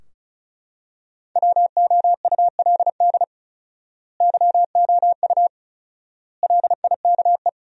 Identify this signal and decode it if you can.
Morse code, heard as a keyed tone, about 36 wpm, 700 Hz. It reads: WOULD YOU LIKE
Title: 4x Word-Spacing - Sentences from Top 100 Words - 35wpm